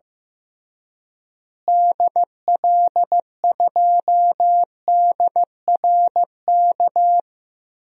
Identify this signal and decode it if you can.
Morse code, heard as a keyed tone, about 15 wpm, 700 Hz. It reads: DL2DRK
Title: DXCC-100 Call Signs - 6 Characters - Repeated 2x - 15wpm